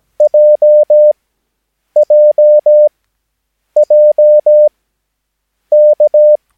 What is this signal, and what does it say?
Morse code, heard as a keyed tone, about 17 wpm, spaced as for 7 wpm, 600 Hz. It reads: JJJK
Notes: Övningsmaterial med alfabetet A till Ö samt siffror 0 till 9 enligt International/Continental Morse Code.